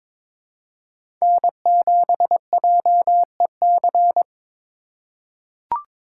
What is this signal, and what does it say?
Morse code, heard as a keyed tone, about 22 wpm, 700 Hz. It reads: N7JEC
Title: USA Call Signs - 5 Characters - 3x Character Spacing - 22wpm